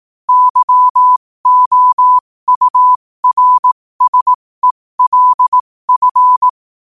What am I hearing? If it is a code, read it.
Morse code, heard as a keyed tone, about 18 wpm, spaced as for 15 wpm, 1000 Hz. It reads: YOURSELF
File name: qotdfiles / 15wpm